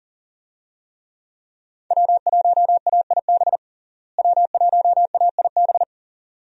Morse code, heard as a keyed tone, about 40 wpm, 700 Hz. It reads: W1AIB W1AIB
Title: USA Call Signs - 5 Characters - Repeated 2x - 40wpm